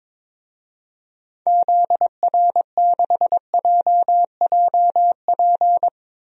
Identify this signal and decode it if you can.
Morse code, heard as a keyed tone, about 22 wpm, 700 Hz. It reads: ZR6JJP
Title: DXCC-100 Call Signs - 6 Characters - 22wpm